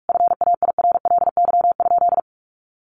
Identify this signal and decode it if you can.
Morse code, heard as a keyed tone, about 44 wpm, 700 Hz. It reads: FAIRLY?
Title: Sentences from Top 1100 Words 45wpm